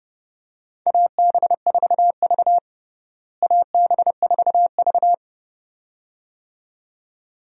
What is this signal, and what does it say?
Morse code, heard as a keyed tone, about 30 wpm, 700 Hz. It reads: AB4V AB4V
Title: USA Call Signs - 4 Characters - Repeated 2x - 30wpm